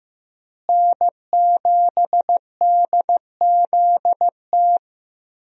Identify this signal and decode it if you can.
Morse code, heard as a keyed tone, about 15 wpm, 700 Hz. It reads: N7DZT
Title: USA Call Signs - 5 Characters - Repeated 2x - 15wpm